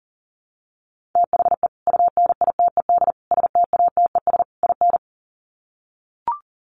Morse code, heard as a keyed tone, about 37 wpm, 700 Hz. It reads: THE UNITED STATES IN
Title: Sets of 4 Words 40wpm